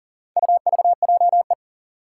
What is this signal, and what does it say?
Morse code, heard as a keyed tone, about 40 wpm, 700 Hz. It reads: UVJE
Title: Four Challenging Letter-Number Combinations 40wpm